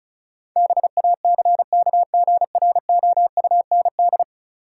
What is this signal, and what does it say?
Morse code, heard as a keyed tone, about 34 wpm, 700 Hz. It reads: BACKGROUND
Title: Sentences from Top 1300 Words - New Word Drill Alt - 35wpm